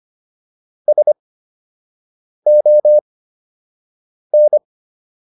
Morse code, heard as a keyed tone, about 25 wpm, spaced as for 6 wpm, 600 Hz. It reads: SON